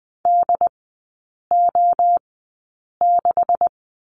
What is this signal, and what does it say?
Morse code, heard as a keyed tone, about 20 wpm, spaced as for 8 wpm, 690 Hz. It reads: DO6
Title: 191 Lesson 31e - Letter v - Three Characters 20wpm